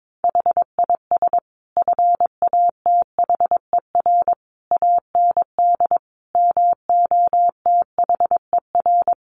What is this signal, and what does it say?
Morse code, heard as a keyed tone, about 22 wpm, 700 Hz. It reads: HIS FATHER AND MOTHER
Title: Sentences from Top 1500 Words 22wpm